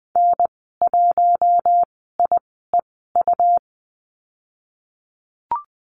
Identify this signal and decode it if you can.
Morse code, heard as a keyed tone, about 20 wpm, spaced as for 15 wpm, 700 Hz. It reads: N1IEU
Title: 167 Lesson 27g - Number 7 - Call Signs 20wpm